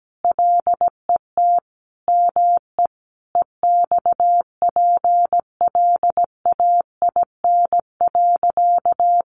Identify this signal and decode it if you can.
Morse code, heard as a keyed tone, about 17 wpm, 700 Hz. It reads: LET ME EXPLAIN.
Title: Sentences from Top 1100 Words 17wpm